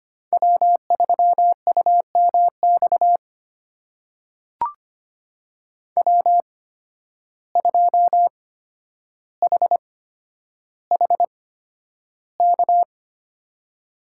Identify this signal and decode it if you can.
Morse code, heard as a keyed tone, about 25 wpm, spaced as for 11 wpm, 700 Hz. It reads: W3UMX W2HHK
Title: USA Call Signs - 5 Characters - 8x Character Spacing - 25wpm